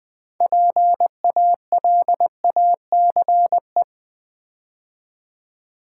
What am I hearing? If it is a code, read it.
Morse code, heard as a keyed tone, about 20 wpm, 700 Hz. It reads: PALACE
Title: 6 Letter Words - First 3 Letters Given - 20wpm